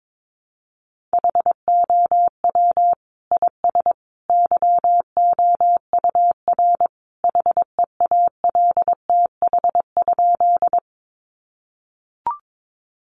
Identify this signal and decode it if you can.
Morse code, heard as a keyed tone, about 22 wpm, 700 Hz. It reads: HOW IS YOUR HEALTH?
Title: Sentences from Top 600 Words 22wpm